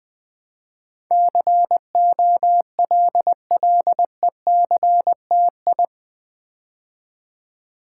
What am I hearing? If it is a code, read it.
Morse code, heard as a keyed tone, about 20 wpm, 700 Hz. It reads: COLLECTI
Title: Build A Word - 4 Letters At A Time - 10-Letter Words - 20wpm